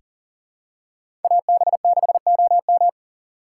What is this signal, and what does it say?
Morse code, heard as a keyed tone, about 40 wpm, 700 Hz. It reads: AB6OM
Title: USA Call Signs - 5 Characters - 2x Character Spacing - 40wpm